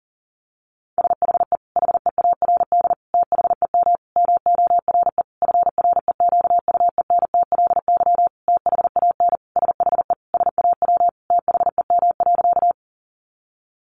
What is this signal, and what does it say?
Morse code, heard as a keyed tone, about 40 wpm, 700 Hz. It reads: SHE HEARD THEM MORE FREQUENTLY THAN SHE SAW THEM.
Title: Sentences from Top 1300 Words 40wpm